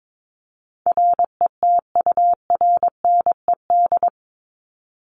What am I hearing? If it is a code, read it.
Morse code, heard as a keyed tone, about 22 wpm, 700 Hz. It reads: RETURNED
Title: ED Suffix - Top 100 Words with Base Words 22wpm